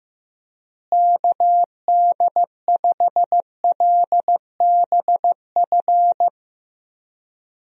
Morse code, heard as a keyed tone, about 15 wpm, 700 Hz. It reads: KD5LBF
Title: USA Call Signs - 6 Characters - Repeated 3x - 15wpm